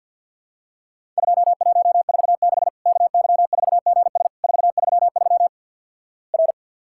Morse code, heard as a keyed tone, about 50 wpm, 700 Hz. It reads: 2146 KY4GS 433 R
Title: Worked All Europe DX Contest - QTC Exchange - 50wpm